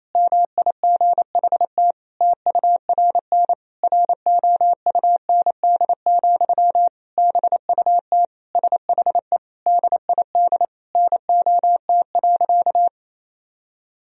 Morse code, heard as a keyed tone, about 28 wpm, 700 Hz. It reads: MIGHT TURN ROUND, BUT SHE DID NOT.
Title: Sentences from Top 1000 Words - Review Part 1 - 28wpm